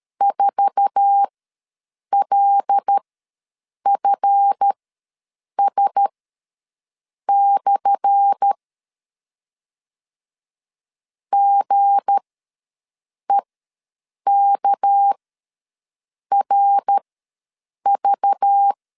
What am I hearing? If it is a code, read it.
Morse code, heard as a keyed tone, about 13 wpm, spaced as for 6 wpm, 800 Hz. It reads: 4LFS/ GEKRV
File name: LEZIONI / 10 SIGNS / 10C DOT ws 12wpm 3548